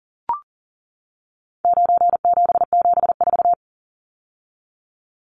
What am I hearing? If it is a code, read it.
Morse code, heard as a keyed tone, about 37 wpm, 700 Hz. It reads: 9774
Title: Numbers - 4-Digit 40wpm